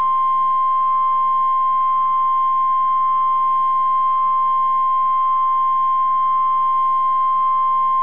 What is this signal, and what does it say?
Emergency Alert System (SAME) tones (the 1050 Hz attention tone). It is heard from 0.3 to 6.9 s.